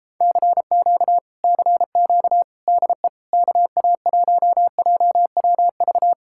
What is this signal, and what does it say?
Morse code, heard as a keyed tone, about 38 wpm, 700 Hz. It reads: CQ CQ DE KA1JWV